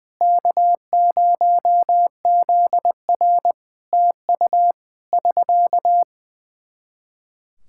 Morse code, heard as a keyed tone, about 19 wpm, 700 Hz. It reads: K0ZR TU <SK>
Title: Practice QSO - Exchange 4 - Ending - 20wpm